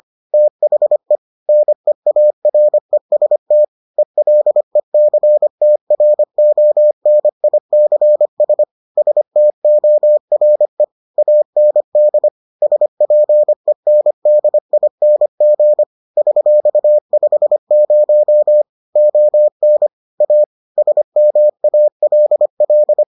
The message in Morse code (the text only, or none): THE NEAREST ELECTRONICS STORE AND SPENDING $50 ON A SMALL